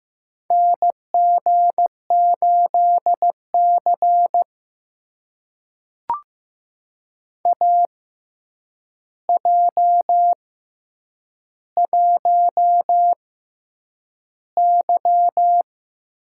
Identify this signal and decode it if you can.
Morse code, heard as a keyed tone, about 15 wpm, spaced as for 8 wpm, 700 Hz. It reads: NG8C AJ1Y
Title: USA Call Signs - 4 Characters - 6x Character Spacing - 15wpm